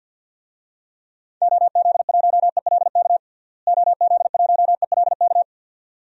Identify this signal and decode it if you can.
Morse code, heard as a keyed tone, about 47 wpm, 700 Hz. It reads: OZ1ELK OZ1ELK
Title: DXCC-100 Call Signs - 6 Characters - Repeated 2x - 50wpm